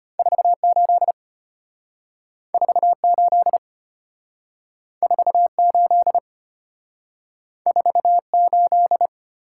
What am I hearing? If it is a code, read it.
Morse code, heard as a keyed tone, about 34 wpm, 700 Hz. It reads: 48 48 48 48